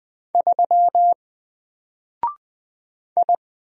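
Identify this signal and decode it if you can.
Morse code, heard as a keyed tone, about 20 wpm, 700 Hz. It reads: 3 I
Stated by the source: Single letter and number Morse drill at 20 wpm, from N4LCD.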